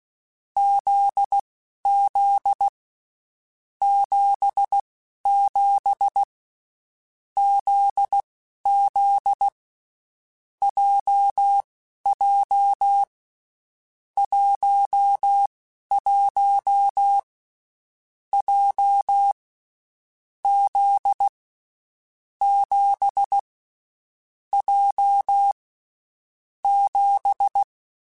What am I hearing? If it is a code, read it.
Morse code, heard as a keyed tone, about 16 wpm, spaced as for 11 wpm, 800 Hz. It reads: ZZ 77 ZZ JJ 11 J Z 7 J 7
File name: Basic Sets / Basic Letter Groups Collective7 / JZ17@10